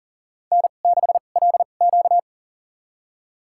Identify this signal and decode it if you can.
Morse code, heard as a keyed tone, about 40 wpm, spaced as for 29 wpm, 700 Hz. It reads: N6LQ
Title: USA Call Signs - 4 Characters - 2x Character Spacing - 40wpm